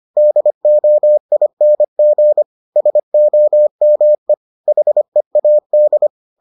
Morse code, heard as a keyed tone, about 25 wpm, 600 Hz. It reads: DOING SOME HEAD